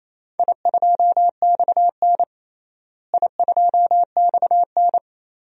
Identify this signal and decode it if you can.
Morse code, heard as a keyed tone, about 28 wpm, 700 Hz. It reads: I2XN I2XN
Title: DXCC-100 Call Signs - 4 Characters - Repeated 2x - 28wpm